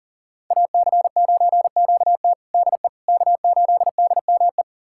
Morse code, heard as a keyed tone, about 37 wpm, 700 Hz. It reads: AC9QT DE K8DME
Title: USA Callsign DE Callsign - 5 Characters - 40wpm